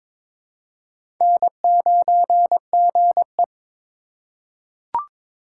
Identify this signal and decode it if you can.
Morse code, heard as a keyed tone, about 22 wpm, 700 Hz. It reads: N9GE
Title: USA Call Signs - 4 Characters - 2x Character Spacing - 22wpm